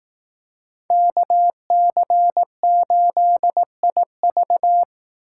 Morse code, heard as a keyed tone, about 19 wpm, 700 Hz. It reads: KC8IV
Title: Speed Racing - USA 2x2 Call Signs 15wpm